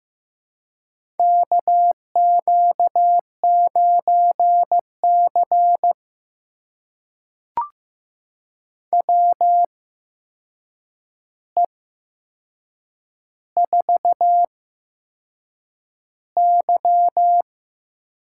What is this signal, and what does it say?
Morse code, heard as a keyed tone, about 15 wpm, spaced as for 6 wpm, 700 Hz. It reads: KQ9C WE4Y
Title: USA Call Signs - 4 Characters - 8x Character Spacing - 15wpm